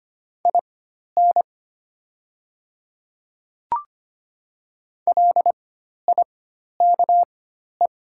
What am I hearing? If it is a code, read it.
Morse code, heard as a keyed tone, about 25 wpm, spaced as for 12 wpm, 700 Hz. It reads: IN LIKE
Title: 4x Character Spacing - Top 100 Words - 25wpm